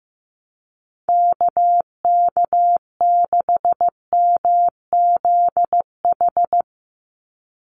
Morse code, heard as a keyed tone, about 15 wpm, 700 Hz. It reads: KK6MZH
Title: USA Call Signs 15wpm